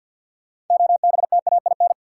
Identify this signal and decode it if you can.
Morse code, heard as a keyed tone, about 49 wpm, 700 Hz. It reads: OBTAIN
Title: Sentences from Top 1300 Words - New Word Drill Alt - 50wpm